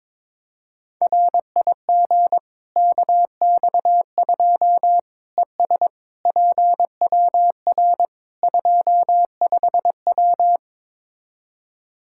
Morse code, heard as a keyed tone, about 22 wpm, 700 Hz. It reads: RIG KX2 ES PWR 25W
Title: Practice QSO - Exchange 2 - Info - 22wpm